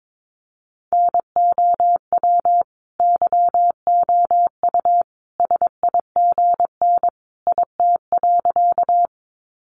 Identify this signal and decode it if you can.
Morse code, heard as a keyed tone, about 22 wpm, 700 Hz. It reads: NOW YOU SIGN IT.
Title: Sentences from Top 1000 Words 22wpm